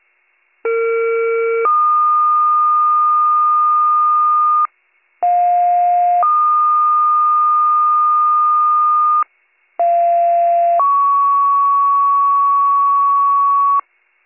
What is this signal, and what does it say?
Two-tone page, heard as a one-second tone, then a three-second tone, about 460 then 1220 Hz, about 710 then 1220 Hz, about 690 then 1120 Hz. It is heard from 0.6 to 13.8 s.